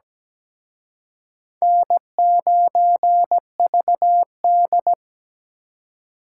Morse code, heard as a keyed tone, about 20 wpm, 700 Hz. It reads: N9VD
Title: Speed Racing - USA Call Signs 17wpm